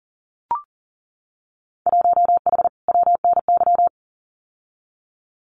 Morse code, heard as a keyed tone, about 38 wpm, 700 Hz. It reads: 1H WNY
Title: Winter Field Day Exchanges Encore 40wpm